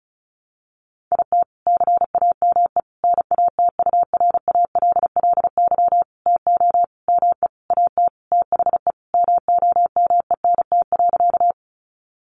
Morse code, heard as a keyed tone, about 34 wpm, 700 Hz. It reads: IT CAME NATURALLY TO ME AT THE MOMENT.
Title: Sentences from Top 1300 Words 35wpm